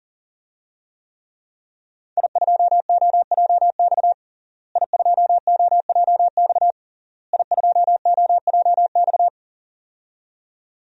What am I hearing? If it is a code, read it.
Morse code, heard as a keyed tone, about 40 wpm, 700 Hz. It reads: I2OJX I2OJX I2OJX
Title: DXCC-100 Call Signs - 5 Characters - Repeated 3x - 40wpm